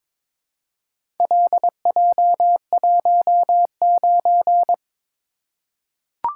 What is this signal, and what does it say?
Morse code, heard as a keyed tone, about 22 wpm, 700 Hz. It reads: LJ19
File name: practice / 22wpm / Four Letter-Number Combinations - 22wpm